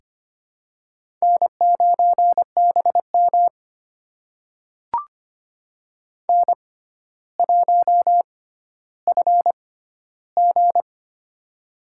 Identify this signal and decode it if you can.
Morse code, heard as a keyed tone, about 25 wpm, spaced as for 13 wpm, 700 Hz. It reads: N9BM N1FG
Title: USA Call Signs - 4 Characters - 6x Character Spacing - 25wpm